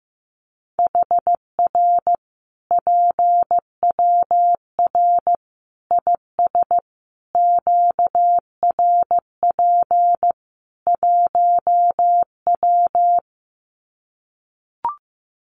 Morse code, heard as a keyed tone, about 15 wpm, 700 Hz. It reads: HR PWR IS QRP 1W
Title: Common QSO Phrases 15wpm